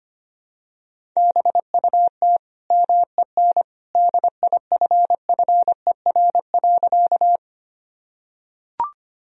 Morse code, heard as a keyed tone, about 25 wpm, 700 Hz. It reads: BUT MEN DIFFER.
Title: Sentences from Top 1000 Words - Review Part 1 - 25wpm